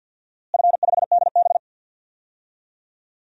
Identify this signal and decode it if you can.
Morse code, heard as a keyed tone, about 47 wpm, 700 Hz. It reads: F5DB
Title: DXCC-100 Call Signs - 4 Characters - Repeated 2x - 50wpm